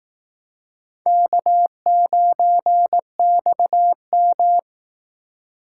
Morse code, heard as a keyed tone, about 18 wpm, 700 Hz. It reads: K9XM
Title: Speed Racing - USA 1x2 Call Signs 15wpm